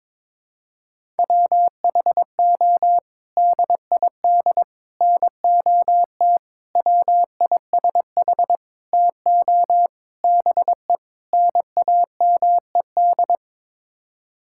Morse code, heard as a keyed tone, about 21 wpm, 700 Hz. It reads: WHO DID NOT WISH TO BE NAMED
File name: practice / 22wpm / Sets of 7 Words - 22wpm